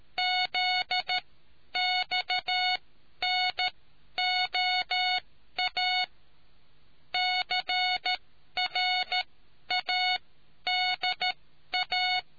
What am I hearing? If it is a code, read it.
Morse code, heard as a keyed tone, about 13 wpm, spaced as for 10 wpm, 730 Hz. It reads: ZXNOA CRADA